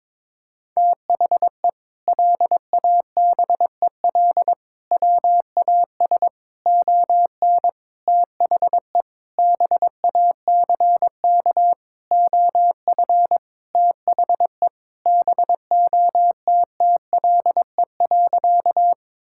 Morse code, heard as a keyed tone, about 22 wpm, 700 Hz. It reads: THE LABEL WAS ON THE BACK OF THE BOTTLE.
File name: practice / 22wpm / Sentences with 4-Word Phrases - 22wpm